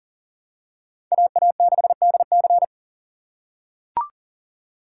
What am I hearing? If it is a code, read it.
Morse code, heard as a keyed tone, about 37 wpm, 700 Hz. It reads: AA6DC
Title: USA Call Signs - 5 Characters - 3x Character Spacing - 40wpm